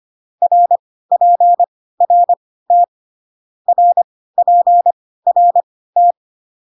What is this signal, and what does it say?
Morse code, heard as a keyed tone, about 25 wpm, spaced as for 16 wpm, 700 Hz. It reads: RPRT RPRT